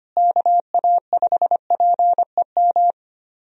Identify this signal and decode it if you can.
Morse code, heard as a keyed tone, about 25 wpm, 700 Hz. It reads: KA5PEM
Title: USA Call Signs - 6 Characters - Repeated 2x - 25wpm